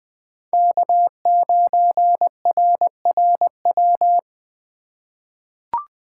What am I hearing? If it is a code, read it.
Morse code, heard as a keyed tone, about 20 wpm, 700 Hz. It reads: K9RRW
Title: USA Call Signs - 2x Character Spacing - 20wpm